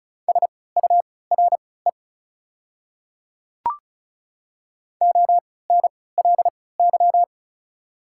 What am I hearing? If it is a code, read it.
Morse code, heard as a keyed tone, about 36 wpm, spaced as for 20 wpm, 690 Hz. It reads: SURE ONLY
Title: 3x Character Spacing - Top 300 Words - 35wpm